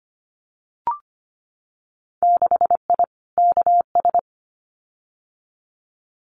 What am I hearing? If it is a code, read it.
Morse code, heard as a keyed tone, about 25 wpm, 700 Hz. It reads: 6I KS
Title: Winter Field Day Exchanges Encore 25wpm